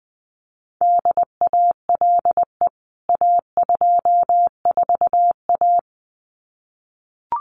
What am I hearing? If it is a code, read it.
Morse code, heard as a keyed tone, about 20 wpm, 700 Hz. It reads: DALE A24A
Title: CWT Exchange 20wpm